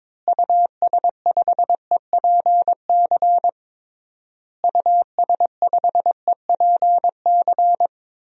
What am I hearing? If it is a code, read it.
Morse code, heard as a keyed tone, about 22 wpm, 700 Hz. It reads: US5EPC US5EPC
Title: DXCC-100 Call Signs - 6 Characters - Repeated 3x - 22wpm